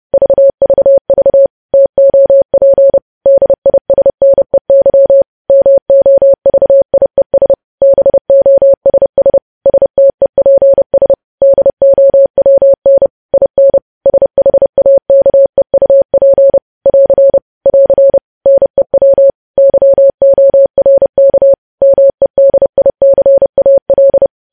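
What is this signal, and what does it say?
Morse code, heard as a keyed tone, about 30 wpm, 550 Hz. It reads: VVV TOP DISNEY MOVIES BOSS STEPS DOWN IN SHAKEUP <AR> <AR> NEW YORK MEDICAL
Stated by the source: News headlines sent in Morse at 30 WPM.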